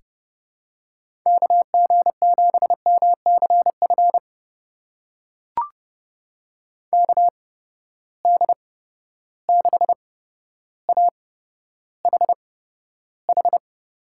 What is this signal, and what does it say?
Morse code, heard as a keyed tone, about 30 wpm, spaced as for 13 wpm, 700 Hz. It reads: KG7MCF KD6AHH
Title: USA Call Signs - 6 Characters - 8x Character Spacing - 30wpm